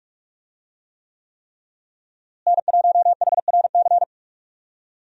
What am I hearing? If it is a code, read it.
Morse code, heard as a keyed tone, about 42 wpm, 700 Hz. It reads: N1HRC
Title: USA Call Signs - 5 Characters - Repeated 3x - 45wpm